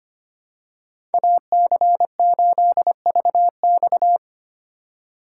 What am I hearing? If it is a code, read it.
Morse code, heard as a keyed tone, about 25 wpm, 700 Hz. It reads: AC8VX
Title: USA Call Signs - 5 Characters - 3x Character Spacing - 25wpm